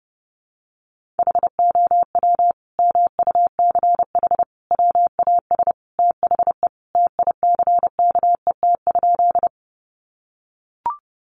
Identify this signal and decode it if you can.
Morse code, heard as a keyed tone, about 30 wpm, 700 Hz. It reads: HOW MUCH WAS THE TICKET?
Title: Sentences from Top 1400 Words 30wpm